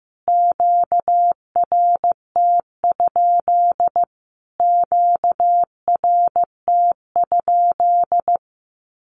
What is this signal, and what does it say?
Morse code, heard as a keyed tone, about 15 wpm, 700 Hz. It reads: QRT? QRT?
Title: Q-Codes Repeated 3x 15wpm